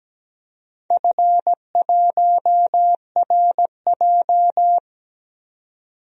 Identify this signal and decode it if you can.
Morse code, heard as a keyed tone, about 17 wpm, 700 Hz. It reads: F1RJ
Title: DXCC-100 Call Signs - 4 Characters - 17wpm